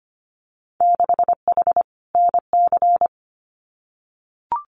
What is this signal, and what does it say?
Morse code, heard as a keyed tone, about 25 wpm, 700 Hz. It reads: 6H NC
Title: Winter Field Day Exchanges Encore 25wpm